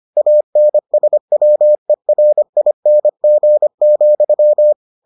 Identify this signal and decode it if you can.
Morse code, heard as a keyed tone, about 25 wpm, 600 Hz. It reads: ANSWERING,